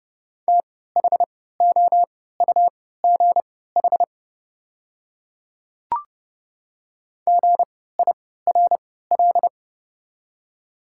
Morse code, heard as a keyed tone, about 30 wpm, spaced as for 17 wpm, 690 Hz. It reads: THOUGH GIRL
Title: 3x Character Spacing - Top 500 Words - 30wpm